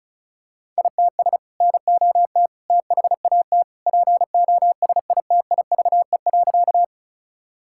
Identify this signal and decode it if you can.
Morse code, heard as a keyed tone, about 36 wpm, 700 Hz. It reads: ITS NOT THAT POSITIVE.
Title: Sentences from Top 1000 Words - Review Part 5 - 35wpm